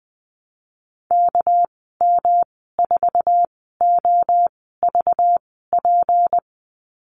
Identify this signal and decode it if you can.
Morse code, heard as a keyed tone, about 20 wpm, spaced as for 14 wpm, 700 Hz. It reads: KM4OVP
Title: 219 Lesson 35g - Number 0 - Call Signs 20wpm